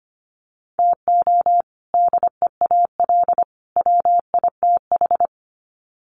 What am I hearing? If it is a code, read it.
Morse code, heard as a keyed tone, about 25 wpm, 700 Hz. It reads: TO DEAL WITH
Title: Sets of 3 Words 25wpm